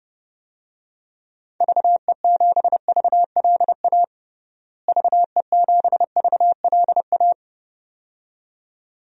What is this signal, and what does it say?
Morse code, heard as a keyed tone, about 31 wpm, 700 Hz. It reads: VE7VLA VE7VLA
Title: DXCC-100 Call Signs - 6 Characters - Repeated 2x - 30wpm